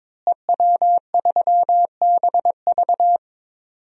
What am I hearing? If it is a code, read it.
Morse code, heard as a keyed tone, about 22 wpm, 700 Hz. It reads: EW3BV
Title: DXCC-100 Call Signs - 5 Characters - Repeated 2x - 22wpm